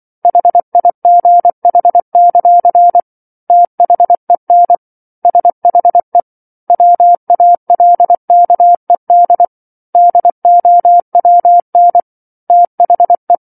Morse code, heard as a keyed tone, about 24 wpm, 700 Hz. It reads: HIGH; THEN SHE WALKED DOWN THE
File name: 07-Chapter-07 24 WPM